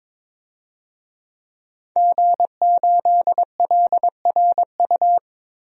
Morse code, heard as a keyed tone, about 21 wpm, 700 Hz. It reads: G8LRU
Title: DXCC-100 Call Signs - 5 Characters - Repeated 3x - 22wpm